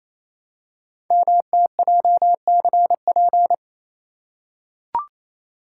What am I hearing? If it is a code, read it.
Morse code, heard as a keyed tone, about 27 wpm, 700 Hz. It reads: MTJCP
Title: Code Groups - 5 Letters - 28wpm